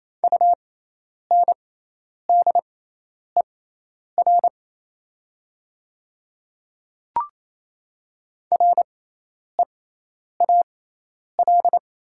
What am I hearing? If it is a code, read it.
Morse code, heard as a keyed tone, about 28 wpm, spaced as for 10 wpm, 700 Hz. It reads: UNDER REAL
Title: 6x Character Spacing - Top 500 Words - 28wpm